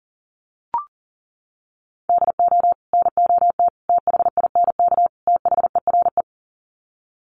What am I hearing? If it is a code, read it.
Morse code, heard as a keyed tone, about 40 wpm, 700 Hz. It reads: DO NOT THINK THERE
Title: Sets of 4 Words 40wpm